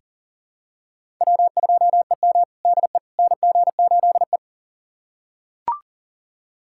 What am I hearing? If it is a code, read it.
Morse code, heard as a keyed tone, about 40 wpm, 700 Hz. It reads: W2EM DE NG8E
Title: USA Callsign DE Callsign - 4 Characters - 40wpm